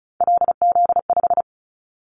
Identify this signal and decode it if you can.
Morse code, heard as a keyed tone, about 36 wpm, 700 Hz. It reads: LZ5
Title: Callsign Prefixes for Top 100 DXCC - 35wpm